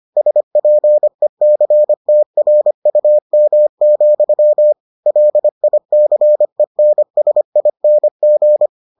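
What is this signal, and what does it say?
Morse code, heard as a keyed tone, about 25 wpm, 600 Hz. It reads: SPECTRUM, LICENSING
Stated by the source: Morse edition of the Foundations of Amateur Radio episodes.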